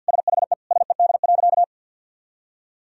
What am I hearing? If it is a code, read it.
Morse code, heard as a keyed tone, about 51 wpm, 700 Hz. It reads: SHE SED.
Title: Sentences from Top 400 Words and Abbreviations 50wpm